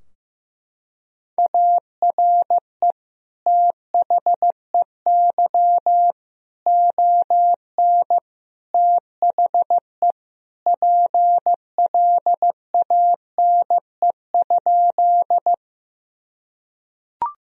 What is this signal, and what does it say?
Morse code, heard as a keyed tone, about 15 wpm, 700 Hz. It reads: ARE THEY ON THE PLANE?
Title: Sentences from Top 1000 Words - Review Part 3 - 15wpm